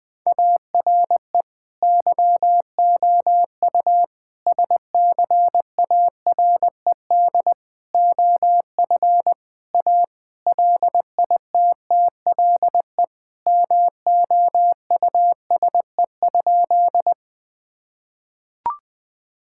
Morse code, heard as a keyed tone, about 20 wpm, 700 Hz. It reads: ARE YOU SCARED OF A LITTLE MOUSE?
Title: Sentences from Top 1000 Words - Review Part 5 - 20wpm